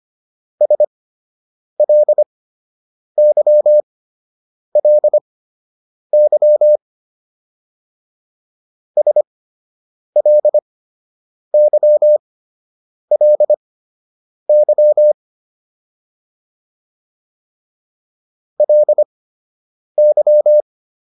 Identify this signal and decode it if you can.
Morse code, heard as a keyed tone, about 25 wpm, spaced as for 8 wpm, 600 Hz. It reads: SLYLY SLYLY LY